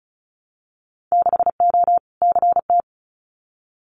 Morse code, heard as a keyed tone, about 34 wpm, 700 Hz. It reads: 6O CT
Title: Winter Field Day Exchanges Encore 35wpm